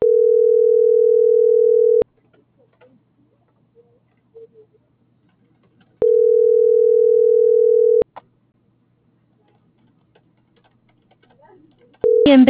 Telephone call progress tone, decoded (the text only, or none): ringback tone